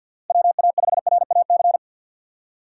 Morse code, heard as a keyed tone, about 51 wpm, 690 Hz. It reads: WA5RAC
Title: USA Call Signs - 6 Characters - 3x Character Spacing - 50wpm